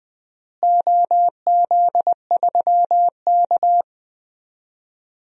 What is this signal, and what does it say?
Morse code, heard as a keyed tone, about 20 wpm, 700 Hz. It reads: OZ3K
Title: DXCC-100 Call Signs - 4 Characters - Repeated 3x - 20wpm